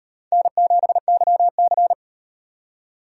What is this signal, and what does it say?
Morse code, heard as a keyed tone, about 38 wpm, 700 Hz. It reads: N7YC